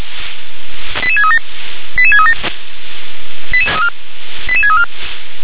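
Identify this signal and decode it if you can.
Five-tone selective call (ZVEI), heard as quick runs of five tones, tones of about 70 ms. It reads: 88637 88637 88634
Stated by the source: ZVEI five-tone selective call received on SDR.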